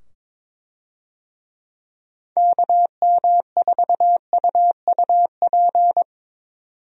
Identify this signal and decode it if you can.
Morse code, heard as a keyed tone, about 22 wpm, 700 Hz. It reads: KM4UUP
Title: USA Call Signs - 6 Characters - 22wpm